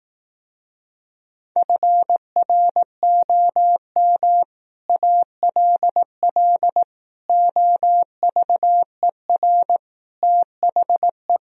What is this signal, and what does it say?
Morse code, heard as a keyed tone, about 18 wpm, 700 Hz. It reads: FROM ALL OVER THE
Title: Speed-Racing - Sets of 5 Words 15wpm